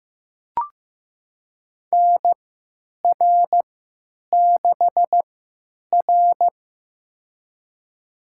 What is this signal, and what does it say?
Morse code, heard as a keyed tone, about 15 wpm, spaced as for 9 wpm, 700 Hz. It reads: NR6R
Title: USA Call Signs - 4 Characters - 3x Character Spacing - 15wpm